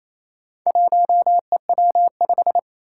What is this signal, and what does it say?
Morse code, heard as a keyed tone, about 29 wpm, 700 Hz. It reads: 1EW5
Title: Four Challenging Letter-Number Combinations 28wpm